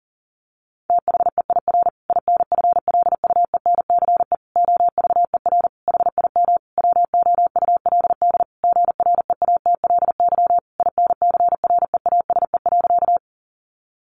Morse code, heard as a keyed tone, about 40 wpm, 700 Hz. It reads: THEIR INFLUENCE OVER HIM WOULD GREATLY INCREASE.
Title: Sentences from Top 1400 Words 40wpm